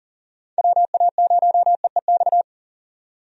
Morse code, heard as a keyed tone, about 37 wpm, 700 Hz. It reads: WA0EEX
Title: USA Call Signs - 6 Characters - 40wpm